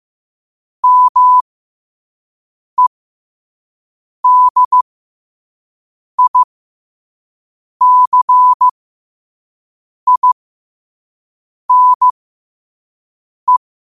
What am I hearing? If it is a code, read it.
Morse code, heard as a keyed tone, about 15 wpm, spaced as for 5 wpm, 1000 Hz. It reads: MEDICINE
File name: qotdfiles / 5wpm